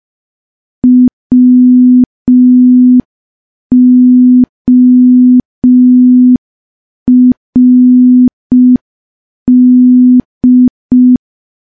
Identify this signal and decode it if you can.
Morse code, heard as a keyed tone, about 5 wpm, 250 Hz. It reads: WORD
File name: morse / morse-training-05WPM